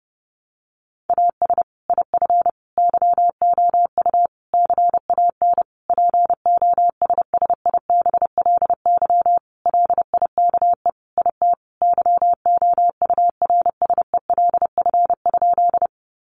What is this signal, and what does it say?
Morse code, heard as a keyed tone, about 30 wpm, 700 Hz. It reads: AS IF YOU CAN POSSIBLY LIKE IT YOURSELF?
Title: Sentences from Top 1300 Words 30wpm